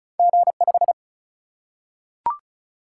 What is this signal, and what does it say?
Morse code, heard as a keyed tone, about 36 wpm, 700 Hz. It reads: G5
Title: Single-Letter Single-Number 35wpm